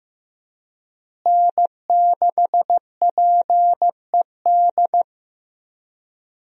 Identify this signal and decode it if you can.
Morse code, heard as a keyed tone, about 16 wpm, 700 Hz. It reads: N6PED
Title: USA Call Signs - 5 Characters - Repeated 2x - 15wpm